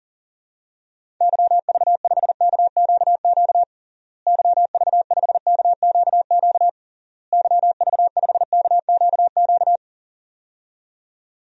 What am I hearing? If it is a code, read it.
Morse code, heard as a keyed tone, about 37 wpm, 700 Hz. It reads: YV5KQQ YV5KQQ YV5KQQ
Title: DXCC-100 Call Signs - 6 Characters - Repeated 3x - 40wpm